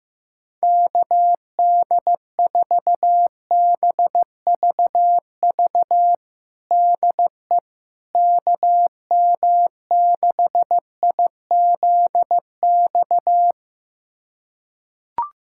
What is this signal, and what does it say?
Morse code, heard as a keyed tone, about 15 wpm, 700 Hz. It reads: KD4BVV DE KM6IZX
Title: USA Callsign DE Callsign - 6 Characters - 15wpm